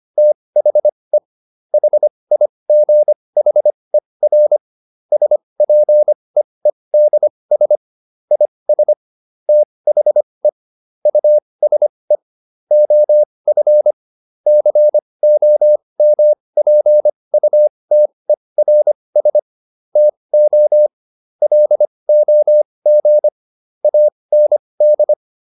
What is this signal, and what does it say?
Morse code, heard as a keyed tone, about 25 wpm, spaced as for 20 wpm, 600 Hz. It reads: THE HIGHER SPEEDS IS THE USE OF COMPUTERS TO LOG AND